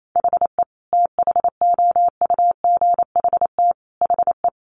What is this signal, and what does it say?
Morse code, heard as a keyed tone, about 28 wpm, 700 Hz. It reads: HE THOUGHT HE
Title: Sentences from Top 600 Words 28wpm